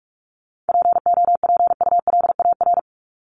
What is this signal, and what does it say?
Morse code, heard as a keyed tone, about 44 wpm, 700 Hz. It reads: POPULAR
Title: Top 501-750 Words 45wpm